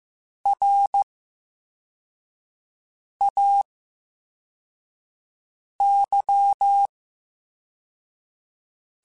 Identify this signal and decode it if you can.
Morse code, heard as a keyed tone, about 15 wpm, spaced as for 4 wpm, 800 Hz. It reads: RAY
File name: Special INGs / Special INGs-mp3s / Slow ing @13